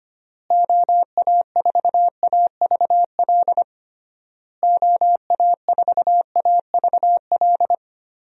Morse code, heard as a keyed tone, about 25 wpm, 700 Hz. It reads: OA4AVL OA4AVL
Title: USA Call Signs - 6 Characters - Repeated 3x - 25wpm